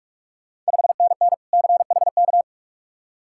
Morse code, heard as a keyed tone, about 46 wpm, 700 Hz. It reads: 5NN CHK
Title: Ontario QSO Party Exchange 45wpm